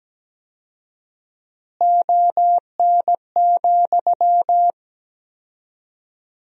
Morse code, heard as a keyed tone, about 17 wpm, 700 Hz. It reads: ON,
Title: 4x Word-Spacing - Sentences from Top 100 Words - 17wpm